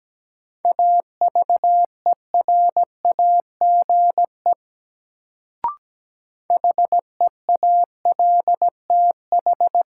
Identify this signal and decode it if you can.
Morse code, heard as a keyed tone, about 17 wpm, 700 Hz. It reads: AVERAGE HEALTH
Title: Sets of 5 Words - New Word Drill - 17wpm